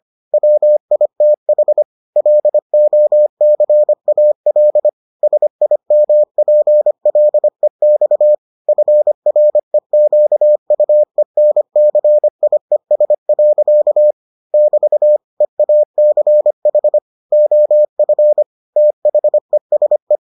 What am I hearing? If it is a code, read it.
Morse code, heard as a keyed tone, about 25 wpm, 600 Hz. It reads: WITH LOCAL SIMPLEX FREQUENCIES. <BT> EACH OF THESE